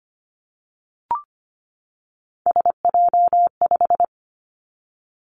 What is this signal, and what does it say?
Morse code, heard as a keyed tone, about 25 wpm, 700 Hz. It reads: SJ5
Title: Three Challenging Letter-Number Combinations 25wpm